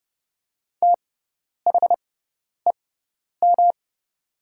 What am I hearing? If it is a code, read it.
Morse code, heard as a keyed tone, about 30 wpm, spaced as for 10 wpm, 700 Hz. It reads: THEM
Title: 6x Character Spacing - Top 300 Words - 30wpm